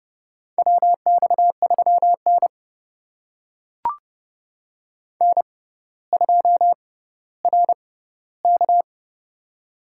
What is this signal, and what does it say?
Morse code, heard as a keyed tone, about 30 wpm, spaced as for 15 wpm, 700 Hz. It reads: WX3N N2RK
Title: USA Call Signs - 4 Characters - 6x Character Spacing - 30wpm